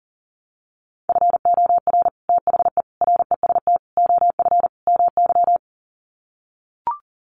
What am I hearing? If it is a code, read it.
Morse code, heard as a keyed tone, about 40 wpm, 700 Hz. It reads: FOR THE REST OF MY
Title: Sets of 5 Words 40wpm